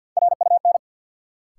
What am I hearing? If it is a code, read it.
Morse code, heard as a keyed tone, about 46 wpm, 700 Hz. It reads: RUN
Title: Sets of 4 Words - New Word Drill - 50wpm